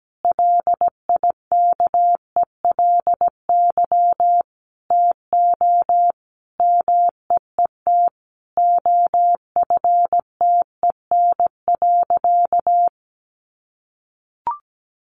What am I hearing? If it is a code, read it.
Morse code, heard as a keyed tone, about 17 wpm, 700 Hz. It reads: LIKELY TO MEET OFTEN.
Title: Sentences from Top 1100 Words 17wpm